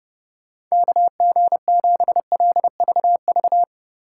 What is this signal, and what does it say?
Morse code, heard as a keyed tone, about 38 wpm, 700 Hz. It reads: KG7LVV